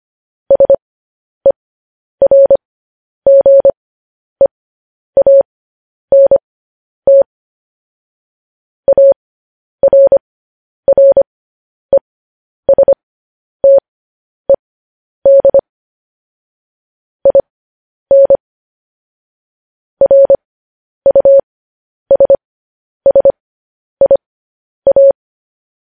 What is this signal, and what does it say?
Morse code, heard as a keyed tone, about 25 wpm, spaced as for 10 wpm, 550 Hz. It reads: SERGEANT ARRESTED IN RUSSIA